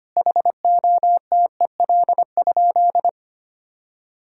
Morse code, heard as a keyed tone, about 25 wpm, 700 Hz. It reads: HOTEL?
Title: Sentences from Top 1600 Words 25wpm